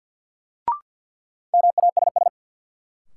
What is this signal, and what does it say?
Morse code, heard as a keyed tone, about 52 wpm, 700 Hz. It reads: MASS
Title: Sets of 5 Words Encore - New Word Drill - 50wpm